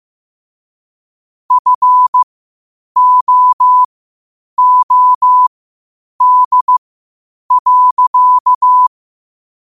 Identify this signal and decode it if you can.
Morse code, heard as a keyed tone, about 15 wpm, spaced as for 8 wpm, 1000 Hz. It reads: FOOD.
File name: qotdfiles / 8wpm